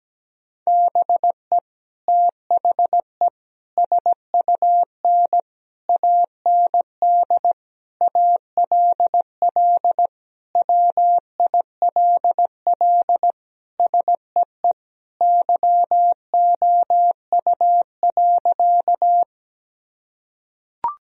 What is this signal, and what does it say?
Morse code, heard as a keyed tone, about 17 wpm, 700 Hz. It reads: BE THE SUN AND ALL WILL SEE YOU.
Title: Sentences from Top 1000 Words - Review Part 1 - 17wpm